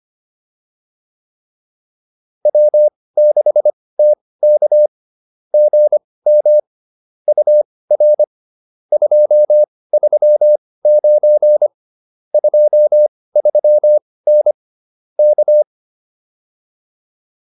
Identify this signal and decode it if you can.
Morse code, heard as a keyed tone, about 25 wpm, spaced as for 18 wpm, 600 Hz. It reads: W6TK GM UR 239 23N K